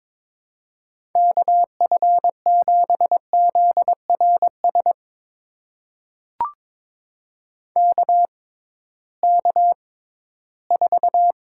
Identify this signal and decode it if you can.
Morse code, heard as a keyed tone, about 22 wpm, spaced as for 14 wpm, 700 Hz. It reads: KF7ZRS KK4
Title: USA Call Signs - 6 Characters - 6x Character Spacing - 22wpm